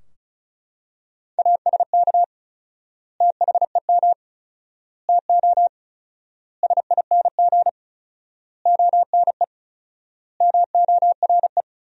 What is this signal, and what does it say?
Morse code, heard as a keyed tone, about 36 wpm, 700 Hz. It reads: ASK THEM TO SING ONE MORE
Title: 4x Word-Spacing - Sentences from Top 400 Words - 35wpm